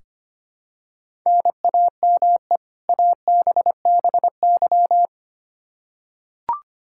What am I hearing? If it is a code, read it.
Morse code, heard as a keyed tone, about 25 wpm, 700 Hz. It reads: NAME ABBY
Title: Practice QSO - Exchange 1 - Beginning - 25wpm